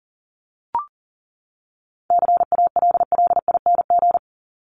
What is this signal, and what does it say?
Morse code, heard as a keyed tone, about 38 wpm, 700 Hz. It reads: CALLING
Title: ING Suffix - Top 100 Words with Base Words 40wpm